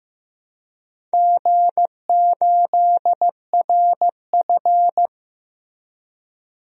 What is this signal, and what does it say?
Morse code, heard as a keyed tone, about 15 wpm, 700 Hz. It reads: G8RF
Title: DXCC-100 Call Signs - 4 Characters - Repeated 3x - 15wpm